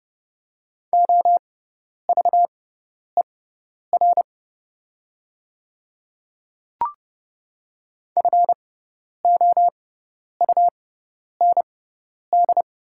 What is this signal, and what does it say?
Morse code, heard as a keyed tone, about 30 wpm, spaced as for 10 wpm, 690 Hz. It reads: OVER FOUND
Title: 6x Character Spacing - Top 500 Words - 30wpm